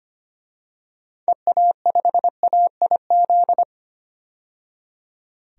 Morse code, heard as a keyed tone, about 25 wpm, 700 Hz. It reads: EA5AIZ
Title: DXCC-100 Call Signs - 6 Characters - 25wpm